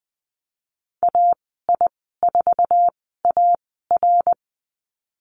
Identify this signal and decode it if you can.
Morse code, heard as a keyed tone, about 20 wpm, 700 Hz. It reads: A I 4 A R
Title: 082 Lesson 14g - Number 2 - Call Signs 20wpm